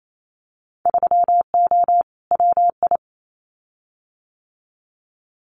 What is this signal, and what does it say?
Morse code, heard as a keyed tone, about 28 wpm, 700 Hz. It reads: 3O WI
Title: Winter Field Day Exchanges 28wpm